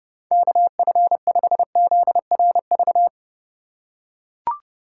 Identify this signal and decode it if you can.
Morse code, heard as a keyed tone, about 30 wpm, 700 Hz. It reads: KF5ZRV
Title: USA Call Signs - 3x Character Spacing - 30wpm